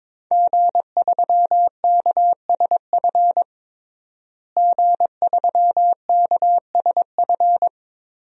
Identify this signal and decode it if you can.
Morse code, heard as a keyed tone, about 22 wpm, 700 Hz. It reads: G3KSF G3KSF
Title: DXCC-100 Call Signs - 5 Characters - Repeated 3x - 22wpm